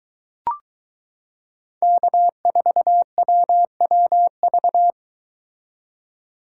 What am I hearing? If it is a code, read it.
Morse code, heard as a keyed tone, about 22 wpm, 700 Hz. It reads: K4WWV